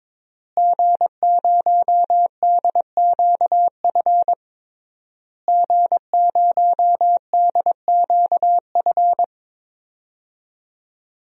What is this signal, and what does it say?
Morse code, heard as a keyed tone, about 22 wpm, 700 Hz. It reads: G0DQF G0DQF
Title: DXCC-100 Call Signs - 5 Characters - Repeated 3x - 22wpm